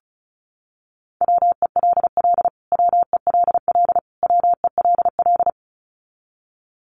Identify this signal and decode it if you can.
Morse code, heard as a keyed tone, about 34 wpm, 700 Hz. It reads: WELL WELL WELL
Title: Top 200 Words Repeated 35wpm